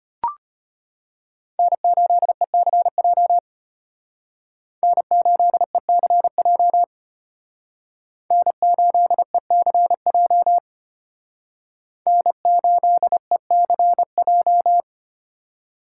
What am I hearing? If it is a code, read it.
Morse code, heard as a keyed tone, about 32 wpm, 700 Hz. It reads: N8ECJ N8ECJ N8ECJ N8ECJ